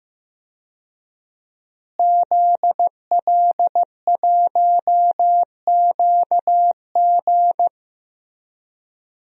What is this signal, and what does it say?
Morse code, heard as a keyed tone, about 15 wpm, 700 Hz. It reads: ZL1QG
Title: DXCC-100 Call Signs - 5 Characters - Repeated 2x - 15wpm